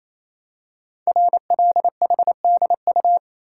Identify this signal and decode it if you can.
Morse code, heard as a keyed tone, about 27 wpm, 700 Hz. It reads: RLHDU
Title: Code Groups - 5 Letters - 28wpm